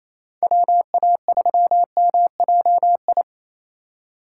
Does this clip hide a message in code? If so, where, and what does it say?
Yes: Morse code from 0.4 to 3.2 s, heard as a keyed tone, about 37 wpm, 700 Hz. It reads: WA3MJI